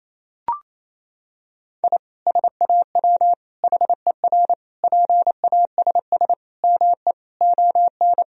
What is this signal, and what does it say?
Morse code, heard as a keyed tone, about 28 wpm, 700 Hz. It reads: I SAW HER PASS ME ON
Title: Sentences from Top 1000 Words - Review Part 2 - 28wpm